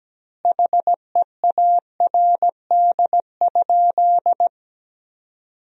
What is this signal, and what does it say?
Morse code, heard as a keyed tone, about 17 wpm, 700 Hz. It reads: HEARD?
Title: 3x Word-Spacing - Sentences from Top 400 Words - 17wpm